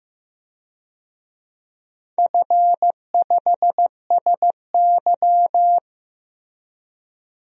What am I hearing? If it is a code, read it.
Morse code, heard as a keyed tone, about 15 wpm, 700 Hz. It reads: F5SY
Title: DXCC-100 Call Signs - 4 Characters - Repeated 3x - 15wpm